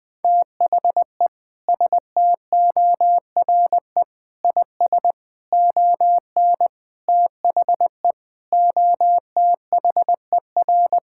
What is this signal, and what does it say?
Morse code, heard as a keyed tone, about 21 wpm, 700 Hz. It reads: THE STORE IS ON THE OTHER
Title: Sentences with 5-Word Phrases - Spoken Beginning - 20wpm